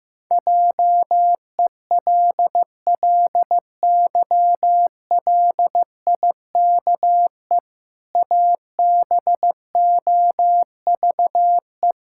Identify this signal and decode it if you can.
Morse code, heard as a keyed tone, about 15 wpm, 700 Hz. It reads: JELLYLIKE ABOVE
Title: Pangram Challenge v2 - 15wpm